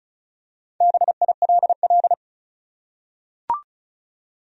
Morse code, heard as a keyed tone, about 36 wpm, 700 Hz. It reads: BILL
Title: USA Common Names - 35wpm